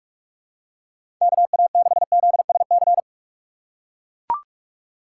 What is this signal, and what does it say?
Morse code, heard as a keyed tone, about 41 wpm, 700 Hz. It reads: KA6ZSC
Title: USA Call Signs - 6 Characters - 4x Character Spacing - 45wpm